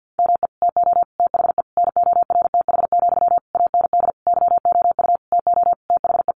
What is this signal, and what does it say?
Morse code, heard as a keyed tone, about 50 wpm, 700 Hz. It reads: ME TO THE NORTH, AND YOU TO THE
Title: Sentences from Top 400 Words 50wpm